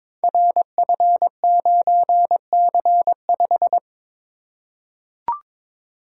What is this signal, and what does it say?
Morse code, heard as a keyed tone, about 22 wpm, 700 Hz. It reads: RF9C5
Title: Code Groups - 5 Letter:Numbers - 22wpm